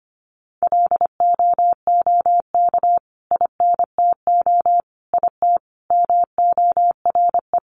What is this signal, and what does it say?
Morse code, heard as a keyed tone, about 25 wpm, 700 Hz. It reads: LOOK INTO IT MORE
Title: Sentences from Top 1300 Words 25wpm